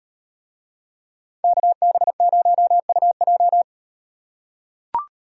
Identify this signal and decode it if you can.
Morse code, heard as a keyed tone, about 38 wpm, 700 Hz. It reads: KB0UJ